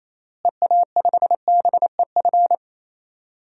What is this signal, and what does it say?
Morse code, heard as a keyed tone, about 27 wpm, 700 Hz. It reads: EA5BEF
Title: DXCC-100 Call Signs - 6 Characters - Repeated 2x - 28wpm